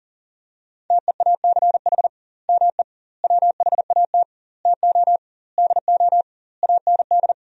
Morse code, heard as a keyed tone, about 40 wpm, 700 Hz. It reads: TEACH ME WHAT TO DO AND
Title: 2x Word-Spacing - Sentences from Top 500 Words - 40wpm